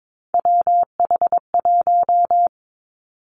Morse code, heard as a keyed tone, about 22 wpm, 700 Hz. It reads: WH1
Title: Three Challenging Letter-Number Combinations 22wpm